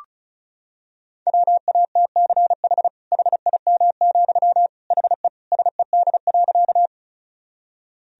Morse code, heard as a keyed tone, about 36 wpm, 700 Hz. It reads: WATCH HIM, HE SED.
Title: Sentences from Top 300 Words and Abbreviations 35wpm